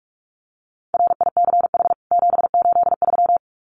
Morse code, heard as a keyed tone, about 44 wpm, 700 Hz. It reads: RICH 783
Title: CWT Exchange 45wpm